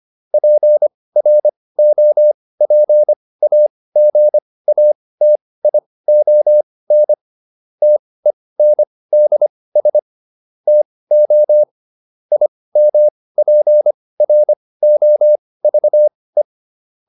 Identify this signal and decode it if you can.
Morse code, heard as a keyed tone, about 25 wpm, spaced as for 18 wpm, 600 Hz. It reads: PROPAGATION TENDS TO IMPROVE
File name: SS211 18